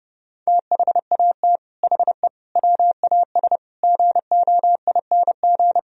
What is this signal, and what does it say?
Morse code, heard as a keyed tone, about 30 wpm, 700 Hz. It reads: THAT HE WAS GOING